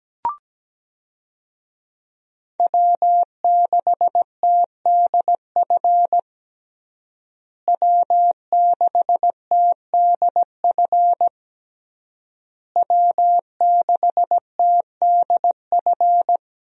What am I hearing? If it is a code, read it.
Morse code, heard as a keyed tone, about 17 wpm, 700 Hz. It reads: W6TDF W6TDF W6TDF
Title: USA Call Signs - 5 Characters - Repeated 3x - 17wpm